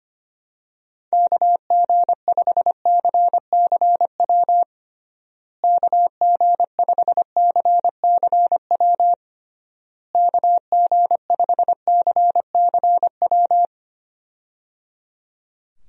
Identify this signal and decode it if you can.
Morse code, heard as a keyed tone, about 25 wpm, 700 Hz. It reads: KG5CCW KG5CCW KG5CCW
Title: USA Call Signs - 6 Characters - Repeated 3x - 25wpm